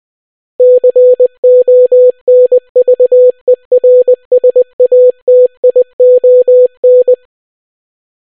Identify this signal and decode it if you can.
Morse code, heard as a keyed tone, about 20 wpm, 500 Hz. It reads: CONVERSATION